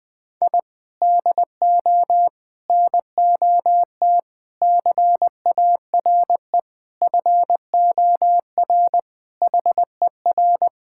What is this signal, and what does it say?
Morse code, heard as a keyed tone, about 21 wpm, 700 Hz. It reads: I DO NOT CARE FOR HER
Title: Sentences from Top 1000 Words - Review Part 6 - 20wpm